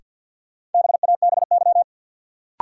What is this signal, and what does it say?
Morse code, heard as a keyed tone, about 52 wpm, 700 Hz. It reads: BABY
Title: Sentences from Top 1300 Words - New Word Drill Alt - 50wpm